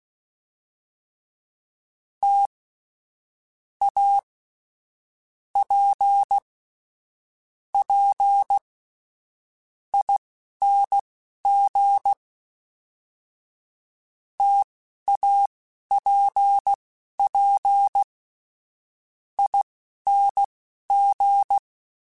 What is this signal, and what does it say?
Morse code, heard as a keyed tone, about 16 wpm, spaced as for 12 wpm, 800 Hz. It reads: T A P P ING TAPP ING
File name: Difficult Combos / Difficult Combos-mp3s / T AOEI @10